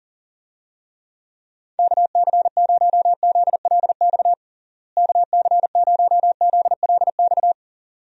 Morse code, heard as a keyed tone, about 40 wpm, 700 Hz. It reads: KC0ZLX KC0ZLX
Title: USA Call Signs - 6 Characters - Repeated 2x - 40wpm